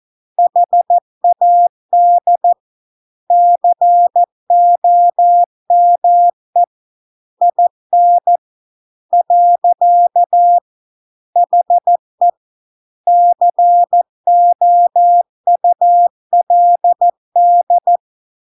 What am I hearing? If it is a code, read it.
Morse code, heard as a keyed tone, about 14 wpm, 700 Hz. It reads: HAD COME IN . HE COULD